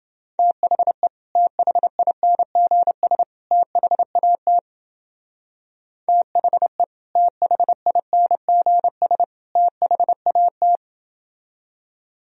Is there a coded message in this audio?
Yes: Morse code from 0.4 to 10.8 s, heard as a keyed tone, about 28 wpm, 700 Hz. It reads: THE THINGS THAT THE THINGS THAT